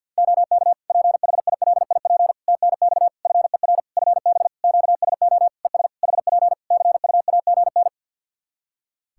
Morse code, heard as a keyed tone, about 50 wpm, 700 Hz. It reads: OK PHILIP TNX FER FB QSO ES HP CUAGN
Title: Practice QSO - Exchange 4 - Ending - 50wpm